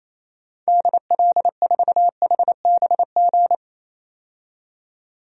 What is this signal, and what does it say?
Morse code, heard as a keyed tone, about 28 wpm, 700 Hz. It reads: DL4HBG
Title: DXCC-100 Call Signs - 6 Characters - 28wpm